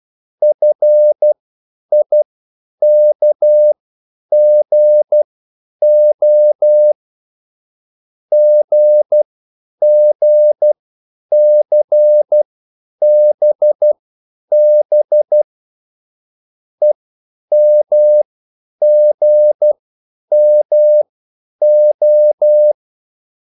Morse code, heard as a keyed tone, about 12 wpm, spaced as for 9 wpm, 600 Hz. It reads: FIKGO GGCBB EMGMO